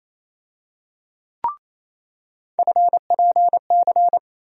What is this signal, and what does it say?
Morse code, heard as a keyed tone, about 28 wpm, 700 Hz. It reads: FPC
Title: USA Call Sign Postfixes - 3 Letters 28wpm